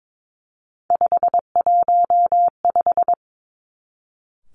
Morse code, heard as a keyed tone, about 22 wpm, 700 Hz. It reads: 515
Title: Three Challenging Letter-Number Combinations 22wpm